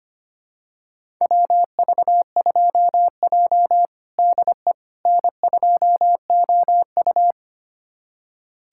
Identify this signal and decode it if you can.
Morse code, heard as a keyed tone, about 25 wpm, 700 Hz. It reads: WV2J DE N2OU
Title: USA Callsign DE Callsign - 4 Characters - 25wpm